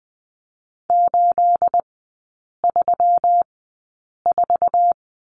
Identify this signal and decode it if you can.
Morse code, heard as a keyed tone, about 20 wpm, spaced as for 8 wpm, 690 Hz. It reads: 834
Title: 212 Lesson 34e - Number 8 - Three Characters 20wpm